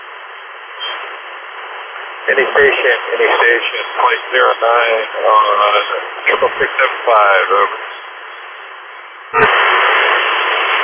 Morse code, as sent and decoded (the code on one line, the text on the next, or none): none